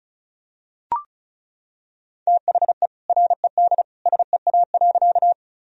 Morse code, heard as a keyed tone, about 34 wpm, 700 Hz. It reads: THE RED SEA.
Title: Sentences from Top 1000 Words - Review Part 2 - 35wpm